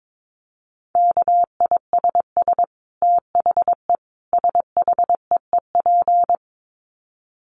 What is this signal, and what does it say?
Morse code, heard as a keyed tone, about 22 wpm, 700 Hz. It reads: KISS THE SHEEP
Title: Dit Challenge 22wpm